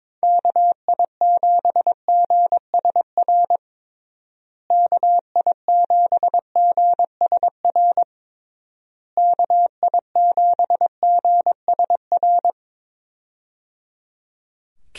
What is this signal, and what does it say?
Morse code, heard as a keyed tone, about 22 wpm, 700 Hz. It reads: KI7GSR KI7GSR KI7GSR
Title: USA Call Signs - 6 Characters - Repeated 3x - 22wpm